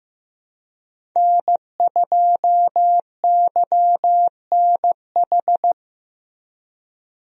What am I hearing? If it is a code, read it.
Morse code, heard as a keyed tone, about 15 wpm, 700 Hz. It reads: N2YNH
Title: USA Call Signs - 5 Characters - Repeated 3x - 15wpm